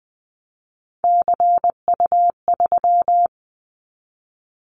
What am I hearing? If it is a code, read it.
Morse code, heard as a keyed tone, about 20 wpm, 700 Hz. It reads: CU3
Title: Callsign Prefixes for Top 50 DXCC - 20wpm